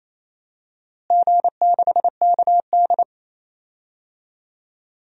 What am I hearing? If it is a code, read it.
Morse code, heard as a keyed tone, about 27 wpm, 700 Hz. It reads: G6KD
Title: DXCC-100 Call Signs - 4 Characters - Repeated 3x - 28wpm